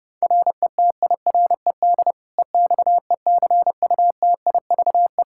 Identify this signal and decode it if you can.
Morse code, heard as a keyed tone, about 31 wpm, 700 Hz. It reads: RETIRED EXECUTIVE
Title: Practice QSO - Exchange 3 - More Info - 30wpm